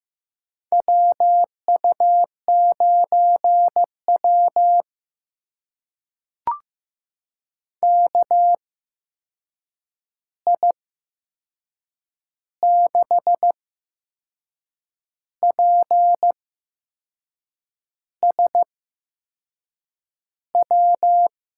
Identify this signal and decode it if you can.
Morse code, heard as a keyed tone, about 15 wpm, spaced as for 6 wpm, 700 Hz. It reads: WU9W KI6PSW
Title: USA Call Signs - 8x Character Spacing - 15wpm